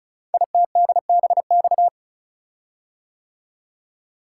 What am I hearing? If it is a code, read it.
Morse code, heard as a keyed tone, about 36 wpm, 690 Hz. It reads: ITDBX
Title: Code Groups - 5 Letters - 35wpm